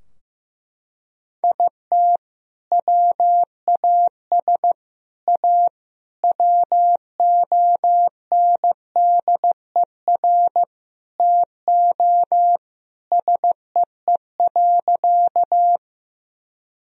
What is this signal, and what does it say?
Morse code, heard as a keyed tone, about 16 wpm, 700 Hz. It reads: IT WAS A WONDER TO SEE.
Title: Sentences from Top 1000 Words - Review Part 3 - 15wpm